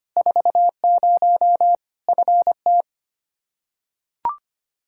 Morse code, heard as a keyed tone, about 25 wpm, 700 Hz. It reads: 40 FT
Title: Practice QSO - Exchange 2 - Info - 25wpm